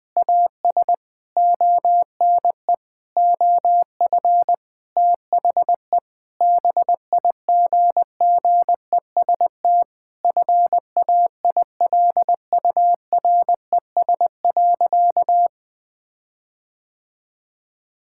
Morse code, heard as a keyed tone, about 21 wpm, 700 Hz. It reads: AS ONE OF THE BIGGEST FAILURES.
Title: Sentences with 7-Word Phrases - 20wpm